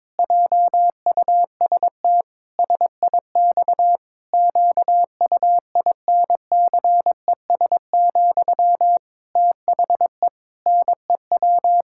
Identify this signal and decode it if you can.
Morse code, heard as a keyed tone, about 22 wpm, 700 Hz. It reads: JUST SIX QUINCES, THE NEW
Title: Pangram Challenge v2 - 22wpm